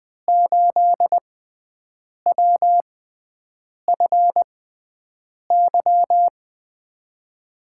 Morse code, heard as a keyed tone, about 20 wpm, spaced as for 7 wpm, 700 Hz. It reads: 8WFY
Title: USA Call Signs - 5 Characters - 6x Character Spacing - 20wpm